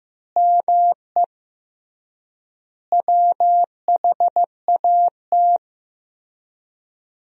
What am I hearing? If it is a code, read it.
Morse code, heard as a keyed tone, about 15 wpm, 700 Hz. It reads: ME WHAT
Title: 3x Word-Spacing - Sentences from Top 500 Words - 15wpm